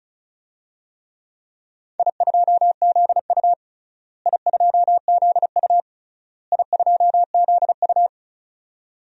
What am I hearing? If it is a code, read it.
Morse code, heard as a keyed tone, about 36 wpm, 700 Hz. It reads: I2ZU I2ZU I2ZU
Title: DXCC-100 Call Signs - 4 Characters - Repeated 3x - 35wpm